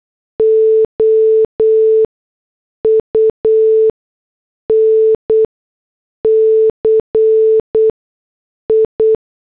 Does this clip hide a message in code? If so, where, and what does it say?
Yes: Morse code from 0.4 to 9.1 s, heard as a keyed tone, about 8 wpm, spaced as for 6 wpm, 440 Hz. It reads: OUNCI